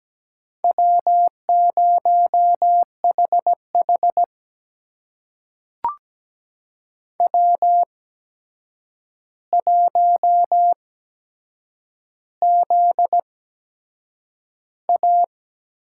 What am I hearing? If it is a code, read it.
Morse code, heard as a keyed tone, about 17 wpm, spaced as for 7 wpm, 700 Hz. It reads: W0HH W1ZA
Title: USA Call Signs - 4 Characters - 8x Character Spacing - 17wpm